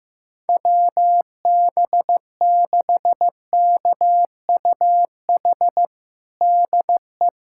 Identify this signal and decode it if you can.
Morse code, heard as a keyed tone, about 15 wpm, 700 Hz. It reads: WB6KUH DE
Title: USA Callsign DE Callsign - 6 Characters - 15wpm